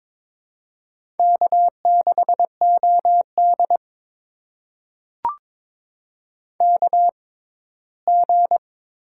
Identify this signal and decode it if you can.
Morse code, heard as a keyed tone, about 22 wpm, spaced as for 15 wpm, 700 Hz. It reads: K6OD KG
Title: USA Call Signs - 6x Character Spacing - 22wpm